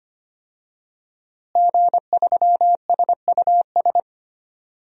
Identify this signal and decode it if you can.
Morse code, heard as a keyed tone, about 25 wpm, 700 Hz. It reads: G3SUS
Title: DXCC-100 Call Signs - 5 Characters - Repeated 2x - 25wpm